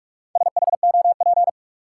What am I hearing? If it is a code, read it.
Morse code, heard as a keyed tone, about 45 wpm, 700 Hz. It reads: SHOP
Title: Sentences from Top 1600 Words - New Word Drill Alt - 45wpm